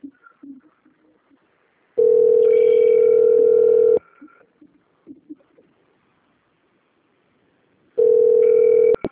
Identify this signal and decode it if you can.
Ringback tone (North America), 440 + 480 Hz.
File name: out- - - - - .28834